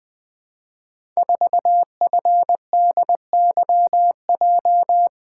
Morse code, heard as a keyed tone, about 20 wpm, 700 Hz. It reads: 4FDYJ
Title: Code Groups - 5 Letter:Numbers - 20wpm